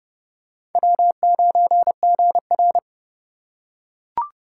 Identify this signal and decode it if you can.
Morse code, heard as a keyed tone, about 28 wpm, 700 Hz. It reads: W9GR